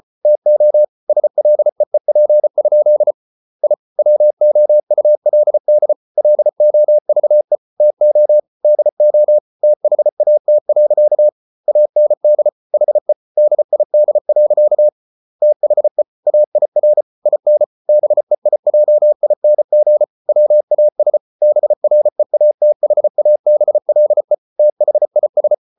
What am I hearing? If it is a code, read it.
Morse code, heard as a keyed tone, about 34 wpm, 600 Hz. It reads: TO SLEEP? I WOULD LOVE TO DO THAT. AND HE DID. THE AIR IN BEIJING WAS BREATHABLE THIS